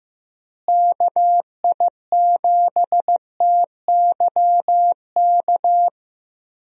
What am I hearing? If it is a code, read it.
Morse code, heard as a keyed tone, about 15 wpm, 700 Hz. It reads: KI7TYK
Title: USA Call Signs - 6 Characters - Repeated 2x - 15wpm